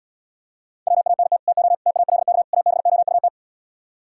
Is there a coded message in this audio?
Yes: Morse code from 0.9 to 3.3 s, heard as a keyed tone, about 77 wpm, 700 Hz. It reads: WANT TO TALK ABOUT